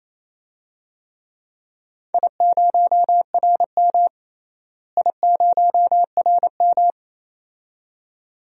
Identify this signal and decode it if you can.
Morse code, heard as a keyed tone, about 28 wpm, 700 Hz. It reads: I0RM I0RM
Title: DXCC-100 Call Signs - 4 Characters - Repeated 2x - 28wpm